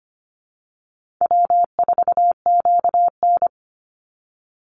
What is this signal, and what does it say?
Morse code, heard as a keyed tone, about 25 wpm, 700 Hz. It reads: W4QN
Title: Call Signs - 25wpm